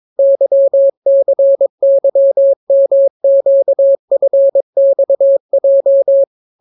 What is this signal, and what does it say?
Morse code, heard as a keyed tone, about 22 wpm, 550 Hz. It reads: YCYMQFXJ